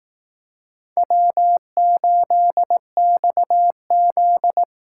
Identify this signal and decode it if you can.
Morse code, heard as a keyed tone, about 18 wpm, 700 Hz. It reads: W8XZ
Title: Speed Racing - USA 1x2 Call Signs 15wpm